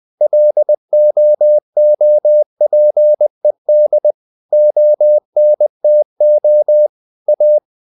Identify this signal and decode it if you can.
Morse code, heard as a keyed tone, about 20 wpm, 600 Hz. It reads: LOOPED ONTO A
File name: the old man and the sea / the old man and the sea part 1 20wpm